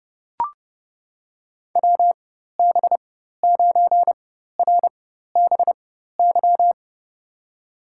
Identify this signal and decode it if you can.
Morse code, heard as a keyed tone, about 32 wpm, spaced as for 14 wpm, 700 Hz. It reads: WB9RBY
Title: USA Call Signs - 6 Characters - 4x Character Spacing - 30wpm